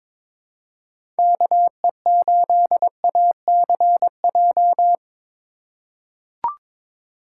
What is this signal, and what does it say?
Morse code, heard as a keyed tone, about 22 wpm, 700 Hz. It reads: KE8ACJ
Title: USA Call Signs - 6 Characters - 2x Character Spacing - 22wpm